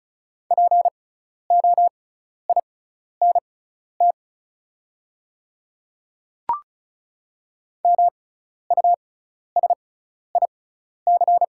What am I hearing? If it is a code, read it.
Morse code, heard as a keyed tone, about 36 wpm, spaced as for 12 wpm, 690 Hz. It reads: POINT MUSIC
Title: 6x Character Spacing - Top 300 Words - 35wpm